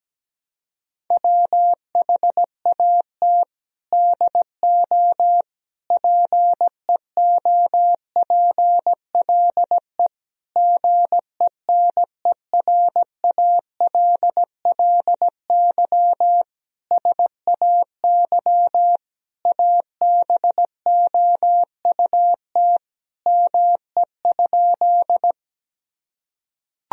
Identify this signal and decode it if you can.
Morse code, heard as a keyed tone, about 17 wpm, 700 Hz. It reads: WHAT DO PEOPLE GENERALLY SAY ABOUT ME?
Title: Sentences from Top 1000 Words - Review Part 4 - 17wpm